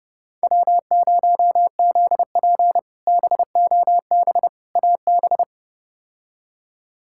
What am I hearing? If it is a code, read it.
Morse code, heard as a keyed tone, about 30 wpm, 700 Hz. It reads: W0ZP BOB AB
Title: K1USN SST QSO Practice - 30wpm